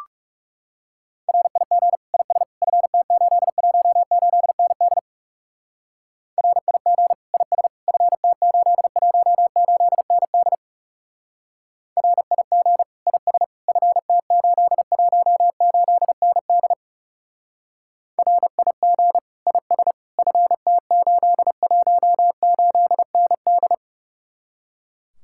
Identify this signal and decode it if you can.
Morse code, heard as a keyed tone, about 37 wpm, 700 Hz. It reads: RIG IS FT818ND RIG IS FT818ND RIG IS FT818ND RIG IS FT818ND
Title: Speed-Racing Common QSO Phrases - 30wpm